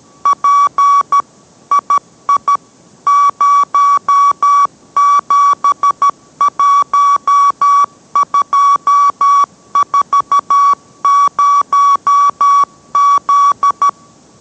Morse code, heard as a keyed tone, about 13 wpm, 1140 Hz. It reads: P II 071240Z